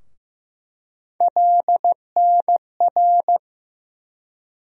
Morse code, heard as a keyed tone, about 15 wpm, 700 Hz. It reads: LNR
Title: USA Call Sign Postfixes - 3 Letters 15wpm